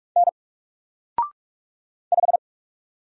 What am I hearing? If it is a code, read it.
Morse code, heard as a keyed tone, about 44 wpm, 700 Hz. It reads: N 5
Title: Single Letter-Number Alt 45wpm